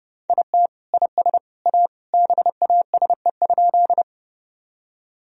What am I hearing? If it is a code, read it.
Morse code, heard as a keyed tone, about 32 wpm, 700 Hz. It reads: IT IS A BASE?
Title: Sentences from Top 1000 Words - Review Part 1 - 30wpm